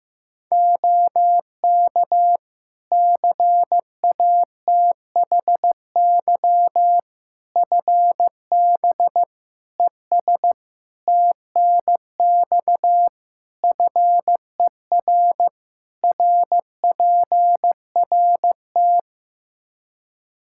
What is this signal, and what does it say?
Morse code, heard as a keyed tone, about 15 wpm, 700 Hz. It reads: OK CATHY FB ES TNX FER RPRT
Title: Practice QSO - Exchange 2 - Info - 15wpm